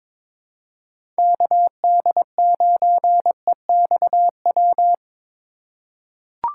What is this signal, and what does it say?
Morse code, heard as a keyed tone, about 21 wpm, 700 Hz. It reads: KD9EXW
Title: USA Call Signs - 6 Characters - 3x Character Spacing - 22wpm